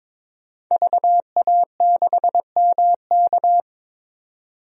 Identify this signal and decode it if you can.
Morse code, heard as a keyed tone, about 22 wpm, 700 Hz. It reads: VA6MK
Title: DXCC-100 Call Signs - 5 Characters - 22wpm